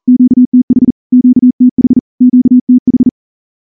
Morse code, heard as a keyed tone, about 38 wpm, 250 Hz. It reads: QTH QTH QTH
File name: morse-training-40WPM